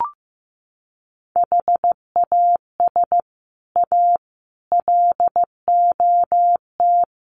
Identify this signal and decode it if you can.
Morse code, heard as a keyed tone, about 15 wpm, 700 Hz. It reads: HAS A LOT
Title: Sets of 5 Words Encore2 15wpm